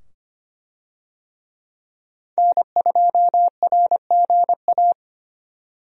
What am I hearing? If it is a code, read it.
Morse code, heard as a keyed tone, about 25 wpm, 700 Hz. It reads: N2RGA
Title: USA Call Signs - 5 Characters - 25wpm